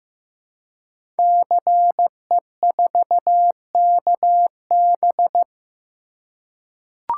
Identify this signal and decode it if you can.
Morse code, heard as a keyed tone, about 15 wpm, 700 Hz. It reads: CE4KB
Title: Code Groups - 5 Letter:Numbers - 15wpm